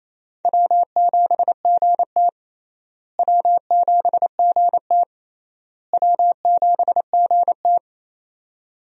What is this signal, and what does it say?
Morse code, heard as a keyed tone, about 28 wpm, 700 Hz. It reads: W7GT W7GT W7GT
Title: USA Call Signs - 4 Characters - Repeated 3x - 28wpm